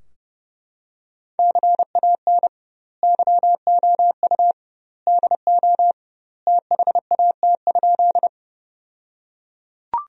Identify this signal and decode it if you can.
Morse code, heard as a keyed tone, about 32 wpm, 700 Hz. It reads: CAN YOU DO THAT?
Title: 2x Word-Spacing - Sentences from Top 100 Words - 30wpm